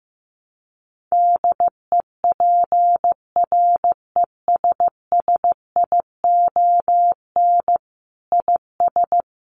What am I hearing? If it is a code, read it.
Morse code, heard as a keyed tone, about 15 wpm, 700 Hz. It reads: DEPRESSION IS
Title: Sentences from Top 1300 Words 15wpm